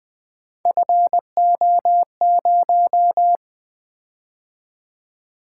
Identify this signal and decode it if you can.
Morse code, heard as a keyed tone, about 20 wpm, 700 Hz. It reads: FO0
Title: Three Letter-Number Combinations - 20wpm